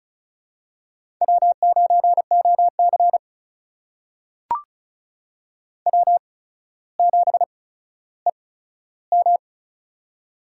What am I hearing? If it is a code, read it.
Morse code, heard as a keyed tone, about 36 wpm, spaced as for 15 wpm, 700 Hz. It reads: W9OC W7EM
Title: USA Call Signs - 4 Characters - 8x Character Spacing - 35wpm